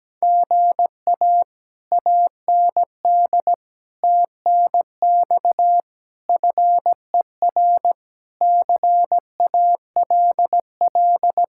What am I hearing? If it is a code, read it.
Morse code, heard as a keyed tone, about 17 wpm, 700 Hz. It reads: GA AND TNX FER CALL
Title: Speed-Racing Common QSO Phrases Encore - 17wpm